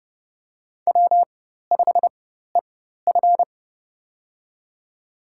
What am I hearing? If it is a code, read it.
Morse code, heard as a keyed tone, about 30 wpm, spaced as for 14 wpm, 700 Hz. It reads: W5EF
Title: USA Call Signs - 4x Character Spacing - 30wpm